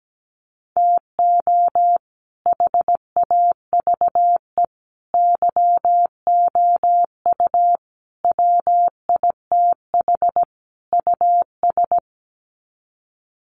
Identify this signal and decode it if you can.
Morse code, heard as a keyed tone, about 17 wpm, 700 Hz. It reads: TO HAVE YOU WITH US
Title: Sets of 5 Words 17wpm